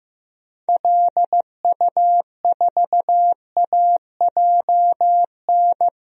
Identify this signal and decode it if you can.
Morse code, heard as a keyed tone, about 15 wpm, 700 Hz. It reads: LU4AJN
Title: DXCC-100 Call Signs - 6 Characters - Repeated 2x - 15wpm